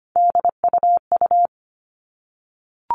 Morse code, heard as a keyed tone, about 25 wpm, 700 Hz. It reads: DUU
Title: All Cut Numbers - 3-Digit 25wpm